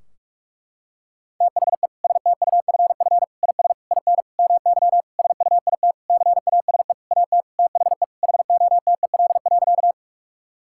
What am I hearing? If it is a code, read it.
Morse code, heard as a keyed tone, about 46 wpm, 700 Hz. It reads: THE STUFF IS IN MY SUIT CASE AT THE HOTEL.
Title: Sentences from Top 1600 Words 45wpm